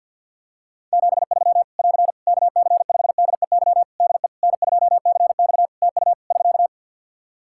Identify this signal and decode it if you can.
Morse code, heard as a keyed tone, about 52 wpm, 700 Hz. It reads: 73 <AR> KC5DEY DE N2CX TU <SK>
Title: Practice QSO - Exchange 4 - Ending - 50wpm